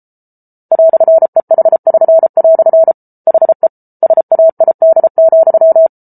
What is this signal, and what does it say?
Morse code, heard as a keyed tone, about 34 wpm, 650 Hz. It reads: "EH<SN>" HE SAID,